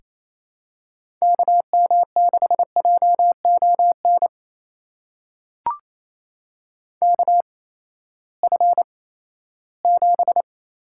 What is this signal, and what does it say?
Morse code, heard as a keyed tone, about 27 wpm, spaced as for 16 wpm, 700 Hz. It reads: KM6JON KF7
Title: USA Call Signs - 6 Characters - 8x Character Spacing - 28wpm